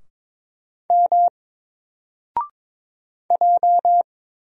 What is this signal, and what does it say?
Morse code, heard as a keyed tone, about 22 wpm, 700 Hz. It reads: M J
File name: Single-Letter-Number-22wpm